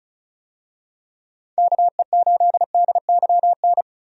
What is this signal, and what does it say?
Morse code, heard as a keyed tone, about 34 wpm, 700 Hz. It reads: KE8DYN
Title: USA Call Signs - 6 Characters - 35wpm